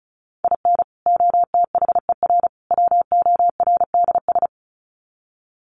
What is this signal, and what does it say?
Morse code, heard as a keyed tone, about 34 wpm, 700 Hz. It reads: IN OTHER WORDS
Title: Sets of 3 Words 35wpm